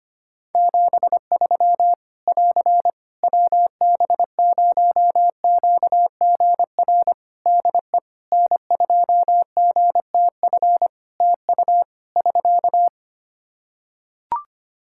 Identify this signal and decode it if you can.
Morse code, heard as a keyed tone, about 25 wpm, 700 Hz. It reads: 73 <AR> WB0QGR DE N2GTF TU <SK>
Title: Practice QSO - Exchange 4 - Ending - 25wpm